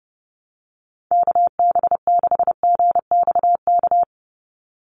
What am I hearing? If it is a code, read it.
Morse code, heard as a keyed tone, about 30 wpm, 700 Hz. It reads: KB6GXK
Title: USA Call Signs 30wpm